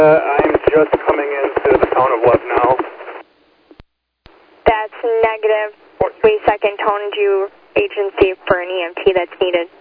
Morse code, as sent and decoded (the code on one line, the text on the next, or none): none